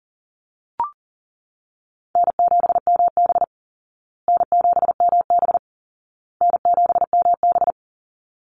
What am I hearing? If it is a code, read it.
Morse code, heard as a keyed tone, about 40 wpm, 700 Hz. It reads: N7MB N7MB N7MB
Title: Easy Call Signs Repeated 3x - 40wpm